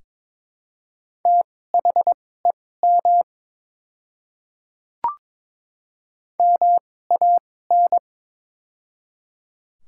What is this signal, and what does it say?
Morse code, heard as a keyed tone, about 22 wpm, spaced as for 16 wpm, 700 Hz. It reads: THEM MAN
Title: 2x Character Spacing - Top 200 Words - 22wpm